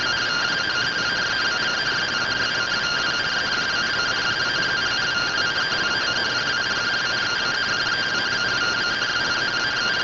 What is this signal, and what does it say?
Two-tone teleprinter signal, about 170 Hz shift.